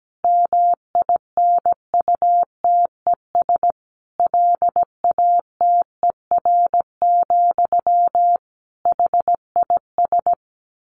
Morse code, heard as a keyed tone, about 17 wpm, 700 Hz. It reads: MINUTES LATER, HIS
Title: A Sound Of Thunder 17wpm